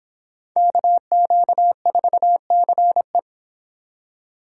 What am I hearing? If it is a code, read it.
Morse code, heard as a keyed tone, about 31 wpm, 700 Hz. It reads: KQ4CE